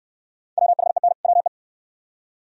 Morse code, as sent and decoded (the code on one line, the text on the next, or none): .-- .... -. / .-- .
WHEN WE